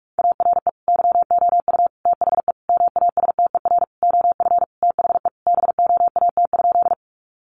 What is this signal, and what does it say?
Morse code, heard as a keyed tone, about 44 wpm, 700 Hz. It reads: ARE YOU THE MASTER OF THE BOAT?
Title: Sentences from Top 1300 Words 45wpm